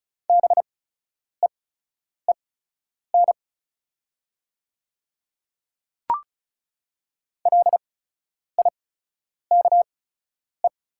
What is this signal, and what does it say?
Morse code, heard as a keyed tone, about 36 wpm, spaced as for 10 wpm, 700 Hz. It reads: BEEN LIKE
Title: 8x Character Spacing - Top 100 Words - 35wpm